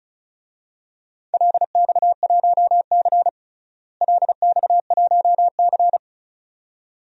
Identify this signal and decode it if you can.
Morse code, heard as a keyed tone, about 36 wpm, 700 Hz. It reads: LX1C LX1C
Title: DXCC-100 Call Signs - 4 Characters - Repeated 2x - 35wpm